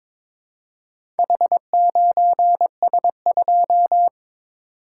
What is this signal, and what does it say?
Morse code, heard as a keyed tone, about 22 wpm, 700 Hz. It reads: H9S2
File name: practice / 22wpm / Four Letter-Number Combinations - 22wpm